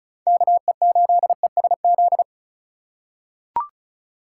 Morse code, heard as a keyed tone, about 34 wpm, 700 Hz. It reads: KE8ESZ
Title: USA Call Signs - 2x Character Spacing - 35wpm